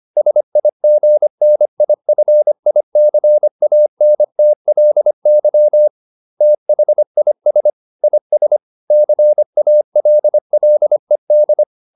Morse code, heard as a keyed tone, about 25 wpm, 600 Hz. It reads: SIGNIFICANTLY THIS IS CALLED